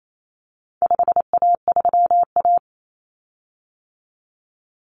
Morse code, heard as a keyed tone, about 28 wpm, 700 Hz. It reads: 5A3A
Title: Common Cut Numbers - 4-Digit 28wpm